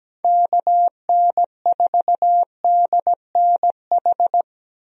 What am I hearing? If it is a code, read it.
Morse code, heard as a keyed tone, about 17 wpm, 700 Hz. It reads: KN4DNH
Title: USA Call Signs - 6 Characters - Repeated 3x - 17wpm